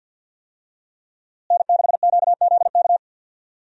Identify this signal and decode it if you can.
Morse code, heard as a keyed tone, about 52 wpm, 700 Hz. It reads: N6QZK
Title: USA Call Signs - 5 Characters - Repeated 3x - 50wpm